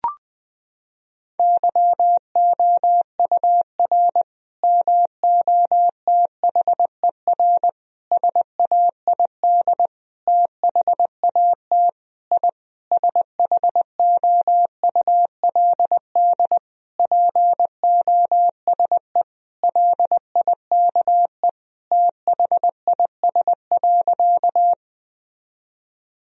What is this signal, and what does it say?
Morse code, heard as a keyed tone, about 20 wpm, 700 Hz. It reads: YOUR MOTHER SAID THAT I SHOULD POSE LIKE THIS.
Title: Sentences from Top 1000 Words - Review Part 2 - 20wpm